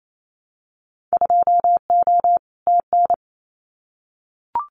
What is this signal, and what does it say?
Morse code, heard as a keyed tone, about 28 wpm, 700 Hz. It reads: 2O TN
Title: Winter Field Day Exchanges 28wpm